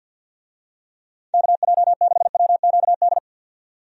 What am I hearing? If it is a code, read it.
Morse code, heard as a keyed tone, about 51 wpm, 700 Hz. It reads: KJ6WQD
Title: USA Call Signs - 6 Characters - 50wpm